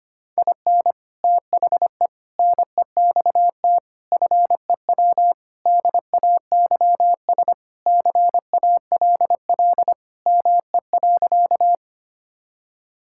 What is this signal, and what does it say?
Morse code, heard as a keyed tone, about 25 wpm, 700 Hz. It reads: IN THE NEXT FEW DAYS CALL ME.
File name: practice / 25wpm / Sentences with 3-Word Phrases - 25wpm